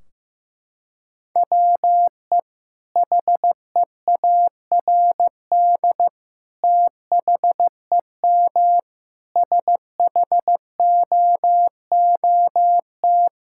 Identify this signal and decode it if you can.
Morse code, heard as a keyed tone, about 16 wpm, 700 Hz. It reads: WE HEARD THEM SHOOT
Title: Sentences from Top 1600 Words 15wpm